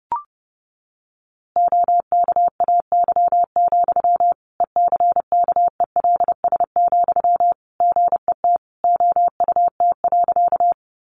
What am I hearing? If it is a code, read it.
Morse code, heard as a keyed tone, about 30 wpm, 700 Hz. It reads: OKAY, ECKELS, GET OUT.
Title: A Sound Of Thunder 30wpm